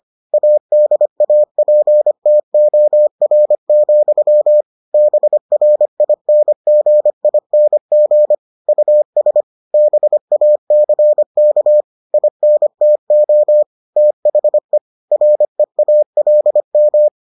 Morse code, heard as a keyed tone, about 25 wpm, 600 Hz. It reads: ADAPTOR, BRINGING US BACK INTO THE REALM